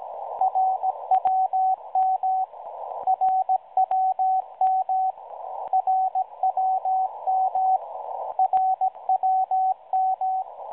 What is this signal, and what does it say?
Morse code, heard as a keyed tone, about 16 wpm, 760 Hz, with radio noise. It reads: RWM RWM RWM RWM
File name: up clips / websdr RWM ID 9996kHz